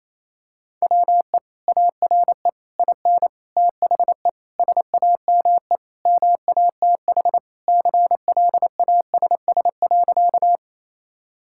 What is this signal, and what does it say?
Morse code, heard as a keyed tone, about 27 wpm, 700 Hz. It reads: WE ARE IN THE SAME MATH CLASS.
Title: Sentences from Top 1000 Words - Review Part 6 - 28wpm